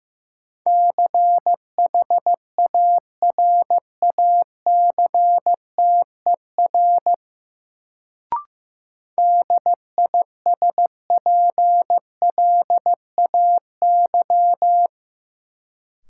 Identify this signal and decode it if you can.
Morse code, heard as a keyed tone, about 15 wpm, 700 Hz. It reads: CHARACTER DISPLAY
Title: Sentences from Top 1200 Words - New Word Drill - 15wpm